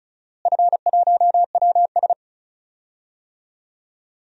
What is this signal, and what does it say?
Morse code, heard as a keyed tone, about 36 wpm, 700 Hz. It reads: F1WS
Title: DXCC-100 Call Signs - 4 Characters - Repeated 2x - 35wpm